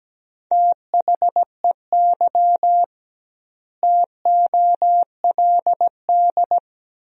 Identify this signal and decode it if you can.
Morse code, heard as a keyed tone, about 17 wpm, 700 Hz. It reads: THEY TOLD
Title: 2x Word-Spacing - Sentences from Top 400 Words - 17wpm